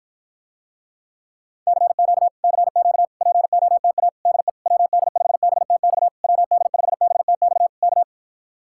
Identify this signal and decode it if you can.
Morse code, heard as a keyed tone, about 54 wpm, 700 Hz. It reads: CQ CQ POTA DE WD5BTX WD5BTX K